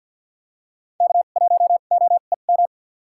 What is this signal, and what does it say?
Morse code, heard as a keyed tone, about 52 wpm, spaced as for 36 wpm, 700 Hz. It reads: K1OEM
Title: USA Call Signs - 5 Characters - 2x Character Spacing - 50wpm